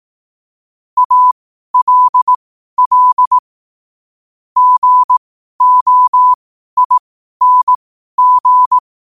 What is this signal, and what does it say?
Morse code, heard as a keyed tone, about 18 wpm, spaced as for 12 wpm, 1000 Hz. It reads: ALL GOING